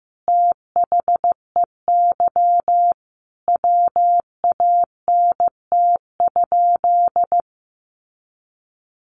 Morse code, heard as a keyed tone, about 15 wpm, 700 Hz. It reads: THEY WANT?
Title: Sentences from Top 200 Words 15wpm